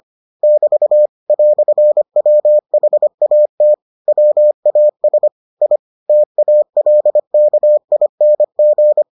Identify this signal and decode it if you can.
Morse code, heard as a keyed tone, about 25 wpm, 600 Hz. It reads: <BT> "WHAT WAS I TALKING